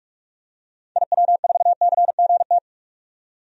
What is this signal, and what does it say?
Morse code, heard as a keyed tone, about 44 wpm, 700 Hz. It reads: IW4CGT
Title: DXCC-100 Call Signs - 6 Characters - 45wpm